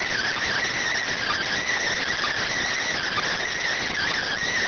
Emergency Alert System (SAME) tones, heard as harsh buzzing bursts. It is heard from 1.1 to 4.3 s.